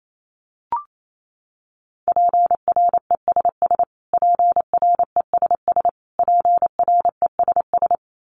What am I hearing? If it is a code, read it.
Morse code, heard as a keyed tone, about 28 wpm, 700 Hz. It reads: PRESS PRESS PRESS
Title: Top 500 Words Repeated 28wpm